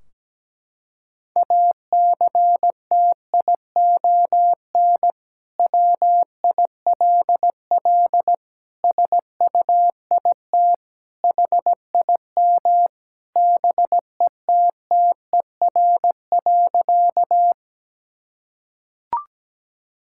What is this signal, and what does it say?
Morse code, heard as a keyed tone, about 17 wpm, 700 Hz. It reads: ACTION WILL SUIT HIM BETTER.
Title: Sentences from Top 1600 Words 17wpm